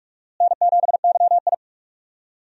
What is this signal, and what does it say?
Morse code, heard as a keyed tone, about 42 wpm, 700 Hz. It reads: N7YI
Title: USA Call Signs - 4 Characters - Repeated 2x - 45wpm